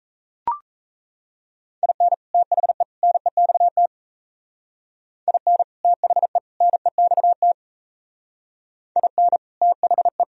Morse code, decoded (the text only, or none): IN THE NEXT IN THE NEXT IN THE